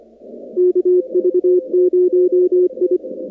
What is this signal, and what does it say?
Morse code, heard as a keyed tone, about 25 wpm, 360 Hz. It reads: KV0I
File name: contests / 2019-cw-wpx-kt5j-k5pi- -2330 0-44